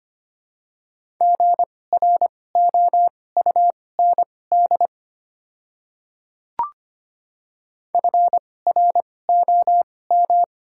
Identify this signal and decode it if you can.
Morse code, heard as a keyed tone, about 25 wpm, spaced as for 18 wpm, 700 Hz. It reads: GROUND FROM
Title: 2x Character Spacing - Top 500 Words - 25wpm